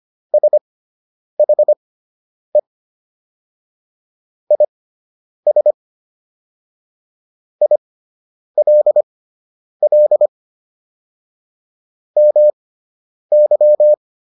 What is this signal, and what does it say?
Morse code, heard as a keyed tone, about 25 wpm, spaced as for 9 wpm, 600 Hz. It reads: SHE IS ILL MY